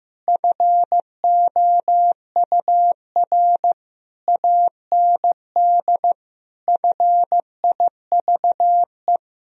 Morse code, heard as a keyed tone, about 15 wpm, 700 Hz. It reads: FOUR AND FIVE
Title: Sentences from Top 1000 Words - Review Part 2 - 15wpm